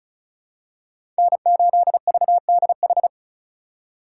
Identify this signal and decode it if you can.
Morse code, heard as a keyed tone, about 36 wpm, 700 Hz. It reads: N8VDH
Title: USA Call Signs - 5 Characters - 2x Character Spacing - 35wpm